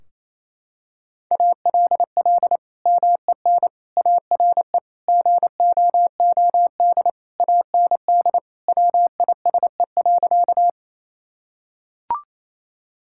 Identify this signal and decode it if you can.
Morse code, heard as a keyed tone, about 28 wpm, 700 Hz. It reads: ALL MEN ARE GOOD AND WISE.
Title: Sentences from Top 1600 Words 28wpm